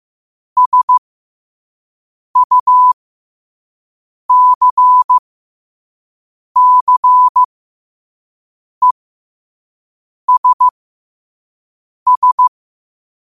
Morse code, heard as a keyed tone, about 15 wpm, spaced as for 5 wpm, 1000 Hz. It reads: SUCCESS
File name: qotdfiles / 5wpm